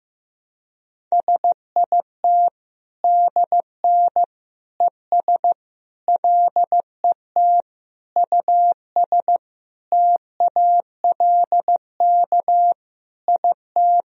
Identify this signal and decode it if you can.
Morse code, heard as a keyed tone, about 16 wpm, 700 Hz. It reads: SIT DN ES LET US TALK IT
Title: Sentences from Top 500 Words and Abbreviations 15wpm